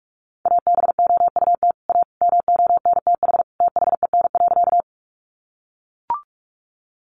Morse code, heard as a keyed tone, about 44 wpm, 700 Hz. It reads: ABOUT A MONTH THEN.
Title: Sentences from Top 800 Words 45wpm